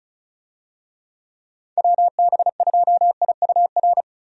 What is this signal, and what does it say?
Morse code, heard as a keyed tone, about 36 wpm, 700 Hz. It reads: WB2IUR
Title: USA Call Signs - 6 Characters - 35wpm